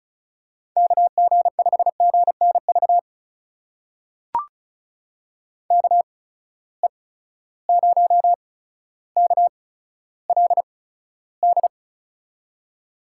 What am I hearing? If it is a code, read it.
Morse code, heard as a keyed tone, about 35 wpm, spaced as for 15 wpm, 700 Hz. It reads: KG5GNV KE0KLD
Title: USA Call Signs - 6 Characters - 8x Character Spacing - 35wpm